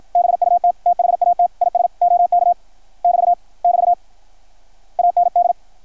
Morse code, heard as a keyed tone, about 53 wpm, 660 Hz. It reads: BUT THAT IS OK <BT> <BT> AND
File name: archived / CRSnet